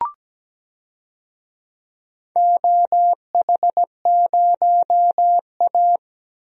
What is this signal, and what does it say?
Morse code, heard as a keyed tone, about 17 wpm, 700 Hz. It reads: OH0A
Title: DXCC-100 Call Signs - 4 Characters - 17wpm